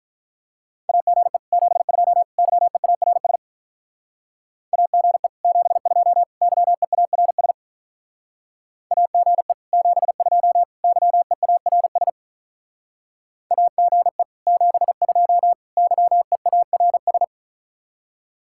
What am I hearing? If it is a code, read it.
Morse code, heard as a keyed tone, about 44 wpm, 700 Hz. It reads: AGE 72 YEARS AGE 72 YEARS AGE 72 YEARS AGE 72 YEARS